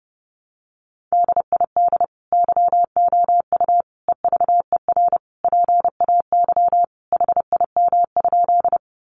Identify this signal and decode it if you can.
Morse code, heard as a keyed tone, about 28 wpm, 700 Hz. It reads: DID YOU EVER PAY HIM?
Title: Sentences from Top 700 Words 30wpm